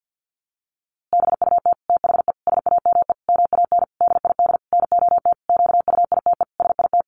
Morse code, heard as a keyed tone, about 50 wpm, 700 Hz. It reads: BUT THE SAME MAN DID NOT QUITE SIT
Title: A Sound Of Thunder 50wpm